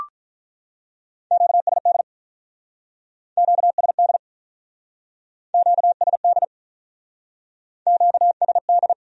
Morse code, heard as a keyed tone, about 45 wpm, 700 Hz. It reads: QSD QSD QSD QSD